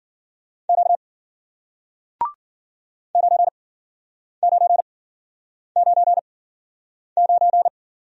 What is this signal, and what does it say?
Morse code, heard as a keyed tone, about 53 wpm, spaced as for 9 wpm, 700 Hz. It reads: Q 9999